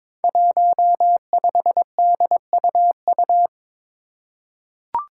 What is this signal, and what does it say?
Morse code, heard as a keyed tone, about 21 wpm, 700 Hz. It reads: 15DUU
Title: Five Challenging Letter-Number Combinations 22wpm